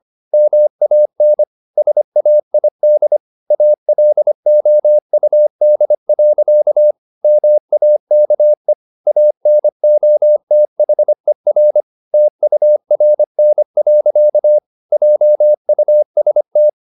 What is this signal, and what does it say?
Morse code, heard as a keyed tone, about 25 wpm, 600 Hz. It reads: MAN SAID ALOUD. MAKE ANOTHER TURN. JUST
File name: the old man and the sea part 1 25wpm